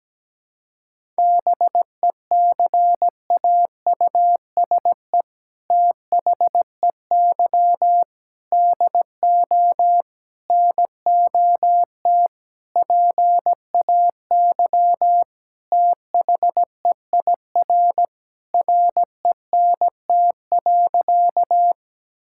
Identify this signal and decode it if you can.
Morse code, heard as a keyed tone, about 17 wpm, 700 Hz. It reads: BECAUSE THEY DO NOT PAY THEIR RENT.
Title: Sentences from Top 1000 Words - Review Part 5 - 17wpm